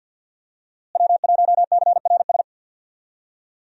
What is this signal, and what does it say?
Morse code, heard as a keyed tone, about 46 wpm, 700 Hz. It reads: W1CRS
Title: USA Call Signs - 5 Characters - Repeated 3x - 50wpm